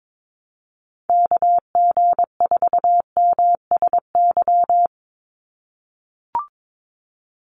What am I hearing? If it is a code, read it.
Morse code, heard as a keyed tone, about 22 wpm, 700 Hz. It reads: KG4MSY
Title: USA Call Signs 22wpm